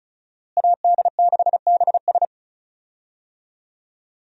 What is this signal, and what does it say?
Morse code, heard as a keyed tone, about 34 wpm, 700 Hz. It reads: AD6BS
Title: USA Call Signs - 5 Characters - Repeated 3x - 35wpm